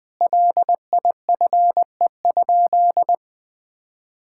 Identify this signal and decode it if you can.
Morse code, heard as a keyed tone, about 20 wpm, 700 Hz. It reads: LIFE?
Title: Sentences from Top 1000 Words - Review Part 5 - 20wpm